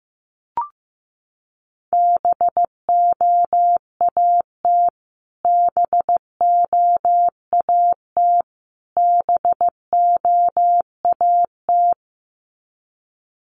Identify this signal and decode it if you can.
Morse code, heard as a keyed tone, about 15 wpm, 700 Hz. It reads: BOAT BOAT BOAT
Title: Top 500 Words Repeated 15wpm